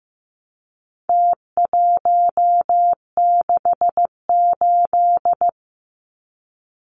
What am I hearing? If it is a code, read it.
Morse code, heard as a keyed tone, about 15 wpm, 700 Hz. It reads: T168
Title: Numbers - Contest Style - 4-Digit 15wpm